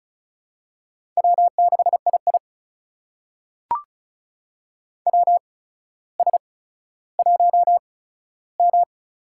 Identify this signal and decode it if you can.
Morse code, heard as a keyed tone, about 34 wpm, spaced as for 15 wpm, 700 Hz. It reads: W6II WS1M
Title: USA Call Signs - 4 Characters - 8x Character Spacing - 35wpm